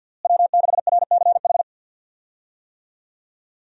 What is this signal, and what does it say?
Morse code, heard as a keyed tone, about 46 wpm, 700 Hz. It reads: W6RCH
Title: USA Call Signs - 5 Characters - Repeated 2x - 50wpm